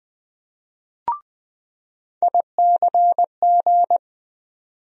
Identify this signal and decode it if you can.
Morse code, heard as a keyed tone, about 20 wpm, 700 Hz. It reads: ICG
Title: USA Call Sign Postfixes - 3 Letters 20wpm